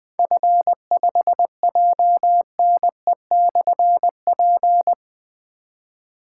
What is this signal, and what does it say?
Morse code, heard as a keyed tone, about 20 wpm, 700 Hz. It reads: F5JNE/P
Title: Difficult Call Signs 20wpm